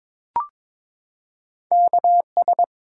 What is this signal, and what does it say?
Morse code, heard as a keyed tone, about 22 wpm, 700 Hz. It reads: KS
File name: ARRL Sections v2 - 22wpm